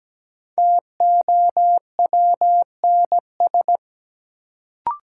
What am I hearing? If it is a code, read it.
Morse code, heard as a keyed tone, about 17 wpm, 700 Hz. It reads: TOWNS
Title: Sentences from Top 1000 Words - Review Part 6 - New Word Drill - 17wpm